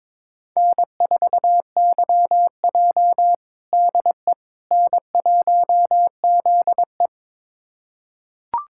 Morse code, heard as a keyed tone, about 21 wpm, 700 Hz. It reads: N4YJ DE N1ZE
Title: USA Callsign DE Callsign - 4 Characters - 22wpm